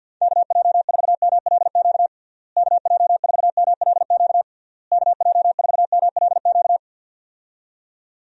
Morse code, heard as a keyed tone, about 52 wpm, 700 Hz. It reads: KJ4MLQ KJ4MLQ KJ4MLQ
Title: USA Call Signs - 6 Characters - Repeated 3x - 50wpm